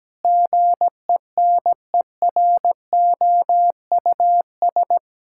Morse code, heard as a keyed tone, about 17 wpm, 700 Hz. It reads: GENEROUS
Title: Pangram Challenge v2 - 17wpm